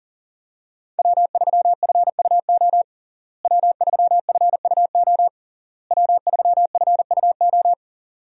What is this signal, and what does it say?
Morse code, heard as a keyed tone, about 37 wpm, 700 Hz. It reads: W3FUO W3FUO W3FUO
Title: USA Call Signs - 5 Characters - Repeated 3x - 40wpm